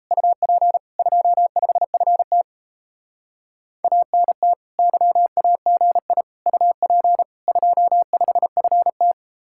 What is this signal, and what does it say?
Morse code, heard as a keyed tone, about 37 wpm, 700 Hz. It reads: UP 25FT ANT YAGI UP 25FT